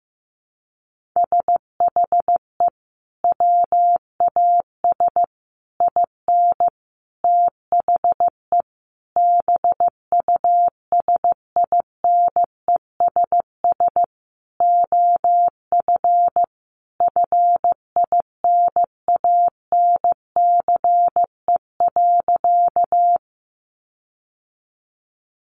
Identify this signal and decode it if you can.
Morse code, heard as a keyed tone, about 15 wpm, 700 Hz. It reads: SHE WAS IN THE BUSINESS OF FINANCE.
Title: Sentences from Top 1500 Words 15wpm